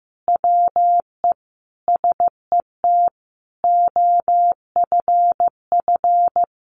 Morse code, heard as a keyed tone, about 15 wpm, 700 Hz. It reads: WE SET OFF
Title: Sentences from Top 1500 Words 15wpm